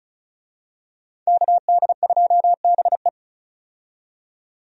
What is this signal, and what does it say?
Morse code, heard as a keyed tone, about 36 wpm, 700 Hz. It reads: KD2BE
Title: USA Call Signs - 5 Characters - 35wpm